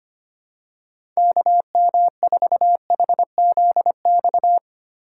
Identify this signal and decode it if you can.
Morse code, heard as a keyed tone, about 25 wpm, 700 Hz. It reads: KM4HZX
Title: USA Call Signs - 6 Characters - 4x Character Spacing - 25wpm